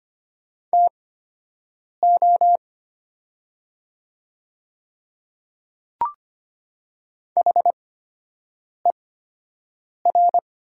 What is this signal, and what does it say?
Morse code, heard as a keyed tone, about 25 wpm, spaced as for 7 wpm, 700 Hz. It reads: TO HER
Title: 8x Character Spacing - Top 100 Words - 25wpm